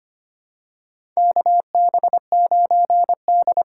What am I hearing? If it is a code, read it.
Morse code, heard as a keyed tone, about 25 wpm, 700 Hz. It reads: KB9D
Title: USA Call Signs - 4 Characters - 4x Character Spacing - 25wpm